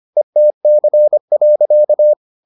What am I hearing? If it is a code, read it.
Morse code, heard as a keyed tone, about 25 wpm, 600 Hz. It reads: ETC.